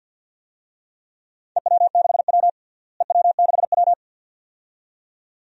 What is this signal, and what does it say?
Morse code, heard as a keyed tone, about 50 wpm, 700 Hz. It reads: EW6W EW6W
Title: DXCC-100 Call Signs - 4 Characters - Repeated 2x - 50wpm